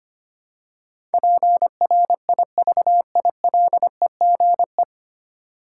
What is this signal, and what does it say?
Morse code, heard as a keyed tone, about 25 wpm, 700 Hz. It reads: PRIVILEGE
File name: practice / 25wpm / 9 Letter Words - First 3 Letters Given - 25wpm